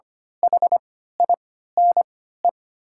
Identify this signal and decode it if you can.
Morse code, heard as a keyed tone, about 25 wpm, spaced as for 14 wpm, 700 Hz. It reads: HINE
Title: 3x Character Spacing - Top 500 Words - 25wpm